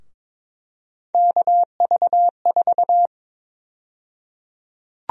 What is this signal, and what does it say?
Morse code, heard as a keyed tone, about 22 wpm, 700 Hz. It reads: KV4
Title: USA Call Sign Prefixes - 22wpm